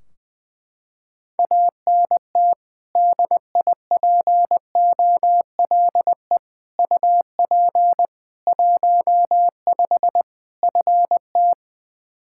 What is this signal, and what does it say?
Morse code, heard as a keyed tone, about 20 wpm, 700 Hz. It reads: ANT DIPOLE UP 15 FT
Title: Practice QSO - Exchange 2 - Info - 20wpm